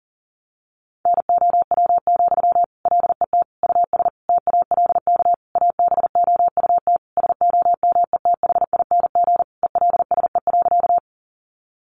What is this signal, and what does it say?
Morse code, heard as a keyed tone, about 40 wpm, 700 Hz. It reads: NOW, LET US TALK ABOUT SOMETHING ELSE.
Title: Sentences from Top 800 Words 40wpm